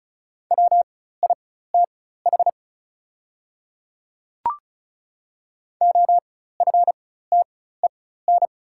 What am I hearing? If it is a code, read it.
Morse code, heard as a keyed tone, about 34 wpm, spaced as for 16 wpm, 690 Hz. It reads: WITH OFTEN
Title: 4x Character Spacing - Top 500 Words - 35wpm